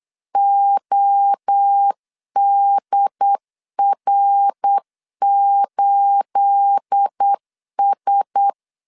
Morse code, heard as a keyed tone, about 8 wpm, 800 Hz. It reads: ODR8S
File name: LEZIONI / 9 28 8wpm 3512